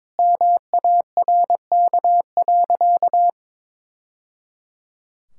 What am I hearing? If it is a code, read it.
Morse code, heard as a keyed tone, about 22 wpm, 700 Hz. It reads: MARK.
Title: Sentences from Top 1000 Words - Review Part 6 - 22wpm